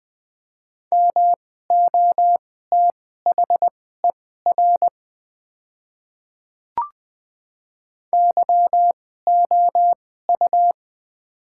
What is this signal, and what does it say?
Morse code, heard as a keyed tone, about 20 wpm, spaced as for 14 wpm, 700 Hz. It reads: MOTHER YOU
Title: 2x Character Spacing - Top 200 Words - 20wpm